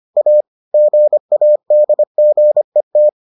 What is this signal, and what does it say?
Morse code, heard as a keyed tone, about 25 wpm, 600 Hz. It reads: A GADGET